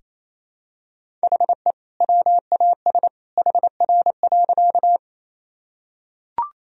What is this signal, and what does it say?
Morse code, heard as a keyed tone, about 28 wpm, 700 Hz. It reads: HE WAS HR.
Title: Sentences from Top 200 Words and Abbreviations 28wpm